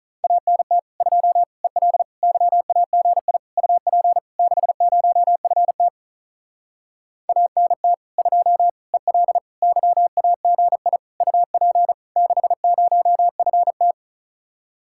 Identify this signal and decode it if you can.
Morse code, heard as a keyed tone, about 45 wpm, 700 Hz. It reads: ANT 2 EL YAGI UP 60FT ANT 2 EL YAGI UP 60FT